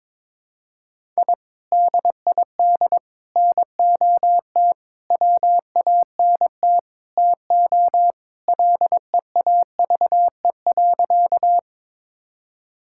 Morse code, heard as a keyed tone, about 22 wpm, 700 Hz. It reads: I DID NOT WANT TO LEAVE.
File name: practice / 22wpm / Sentences with 4-Word Phrases - 22wpm